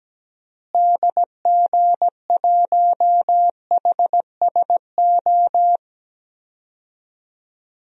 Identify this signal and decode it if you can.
Morse code, heard as a keyed tone, about 17 wpm, 700 Hz. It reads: DG1HSO
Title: DXCC-100 Call Signs - 6 Characters - Repeated 2x - 17wpm